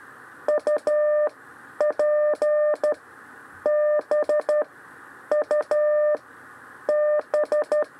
Morse code, heard as a keyed tone, about 12 wpm, spaced as for 8 wpm, 600 Hz. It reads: UPBUB
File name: 6 lezione cw 2